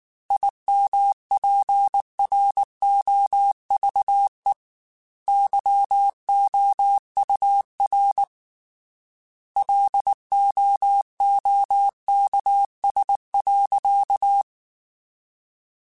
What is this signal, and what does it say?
Morse code, heard as a keyed tone, about 19 wpm, 800 Hz. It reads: IMPROVE YOUR LOOKS.
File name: Quotes / Quotes-mp3s / A Smile @18